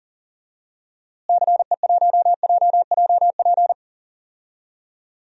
Morse code, heard as a keyed tone, about 40 wpm, 700 Hz. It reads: CE1JJP
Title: DXCC-100 Call Signs - 6 Characters - Repeated 2x - 40wpm